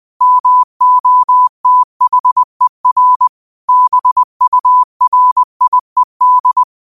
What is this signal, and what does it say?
Morse code, heard as a keyed tone, about 20 wpm, 1000 Hz. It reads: MOTHER BURIED